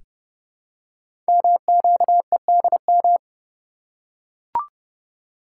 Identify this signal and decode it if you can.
Morse code, heard as a keyed tone, about 29 wpm, 700 Hz. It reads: MQEDM
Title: Code Groups - 5 Letter:Numbers - 30wpm